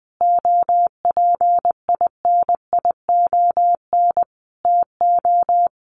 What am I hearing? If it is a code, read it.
Morse code, heard as a keyed tone, about 19 wpm, 700 Hz. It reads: OPINION TO
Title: Sentences from Top 1300 Words 20wpm